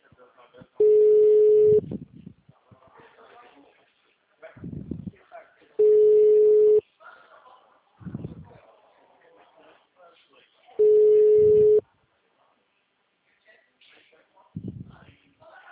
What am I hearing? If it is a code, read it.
Ringback tone (Europe, 425 Hz), with speech.
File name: out- -2002- - - .33468